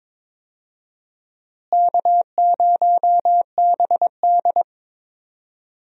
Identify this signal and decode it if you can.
Morse code, heard as a keyed tone, about 22 wpm, 700 Hz. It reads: K0BD
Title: USA Call Signs - 4 Characters - Repeated 2x - 22wpm